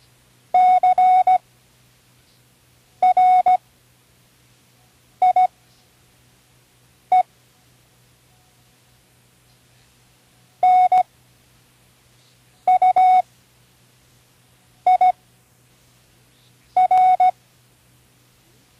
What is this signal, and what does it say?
Morse code, heard as a keyed tone, about 17 wpm, spaced as for 5 wpm, 740 Hz. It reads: CRIE NUIR
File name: CWResources / Learning Morse Code (from Tune in the World) / 10 random